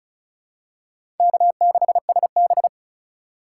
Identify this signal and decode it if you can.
Morse code, heard as a keyed tone, about 36 wpm, 700 Hz. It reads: K6SB
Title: USA Call Signs - 4 Characters - 2x Character Spacing - 35wpm